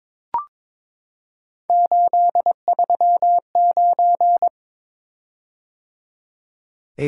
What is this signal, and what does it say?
Morse code, heard as a keyed tone, about 22 wpm, 700 Hz. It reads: 839
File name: practice / 22wpm / Three Letter-Number Combinations - 22wpm